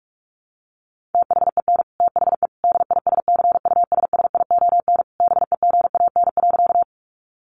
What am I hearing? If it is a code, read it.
Morse code, heard as a keyed tone, about 44 wpm, 700 Hz. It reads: THEN THE DISCUSSION BEGAN.
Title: Sentences from Top 1100 Words 45wpm